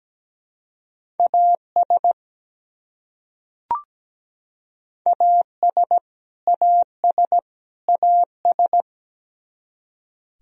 Morse code, heard as a keyed tone, about 17 wpm, 700 Hz. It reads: AS AS AS AS
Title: Top 100 Words Repeated 17wpm